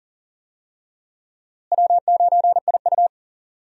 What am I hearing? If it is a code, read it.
Morse code, heard as a keyed tone, about 40 wpm, 700 Hz. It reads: W9IU
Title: USA Call Signs - 4 Characters - Repeated 2x - 40wpm